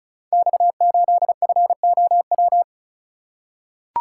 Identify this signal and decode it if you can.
Morse code, heard as a keyed tone, about 36 wpm, 690 Hz. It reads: X8FOW
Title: Code Groups - 5 Letter:Numbers - 35wpm